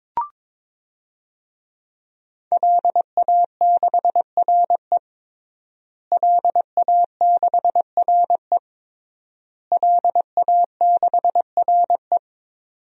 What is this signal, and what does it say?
Morse code, heard as a keyed tone, about 21 wpm, 700 Hz. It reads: LA6RE LA6RE LA6RE
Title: DXCC-100 Call Signs - 5 Characters - Repeated 3x - 22wpm